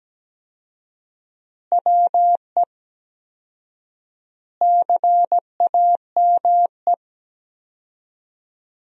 Morse code, heard as a keyed tone, about 17 wpm, 700 Hz. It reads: WE CAME
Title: 4x Word-Spacing - Sentences from Top 400 Words - 17wpm